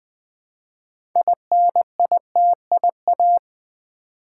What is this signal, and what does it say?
Morse code, heard as a keyed tone, about 20 wpm, 700 Hz. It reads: INITIA
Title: Build A Word - 3 Letters At A Time - 9-Letter Words - 20wpm